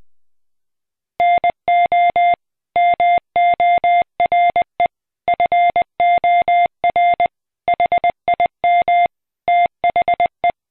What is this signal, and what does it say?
Morse code, heard as a keyed tone, about 20 wpm, 700 Hz. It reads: NO MORE FOR HIM THE